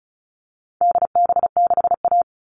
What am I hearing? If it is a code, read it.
Morse code, heard as a keyed tone, about 34 wpm, 700 Hz. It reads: DB6A
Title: All Cut Numbers - 4-Digit 35wpm